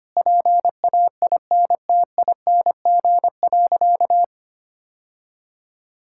Morse code, heard as a keyed tone, about 25 wpm, 700 Hz. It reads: PAINTING.
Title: Sentences from Top 1000 Words - Review Part 6 - 25wpm